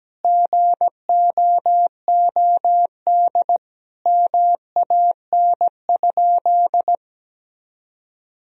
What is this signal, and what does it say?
Morse code, heard as a keyed tone, about 17 wpm, 700 Hz. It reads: GOOD MAN?
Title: Sentences from Top 1000 Words - Review Part 4 - 17wpm